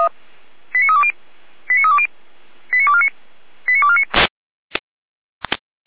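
Five-tone selective call (ZVEI), heard as quick runs of five tones, tones of about 70 ms. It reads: 89320 89320 89249 89249
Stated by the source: ZVEI five-tone call received on an SDR.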